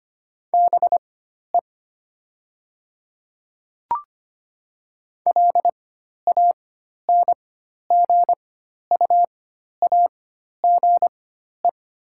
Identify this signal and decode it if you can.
Morse code, heard as a keyed tone, about 25 wpm, spaced as for 12 wpm, 690 Hz. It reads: BE LANGUAGE
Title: 4x Character Spacing - Top 500 Words - 25wpm